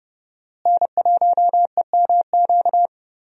Morse code, heard as a keyed tone, about 30 wpm, 700 Hz. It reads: N1EMQ
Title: USA Call Signs - 5 Characters - Repeated 2x - 30wpm